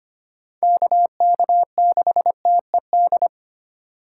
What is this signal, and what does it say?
Morse code, heard as a keyed tone, about 25 wpm, 700 Hz. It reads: KK6TED
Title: USA Call Signs - 6 Characters - 2x Character Spacing - 25wpm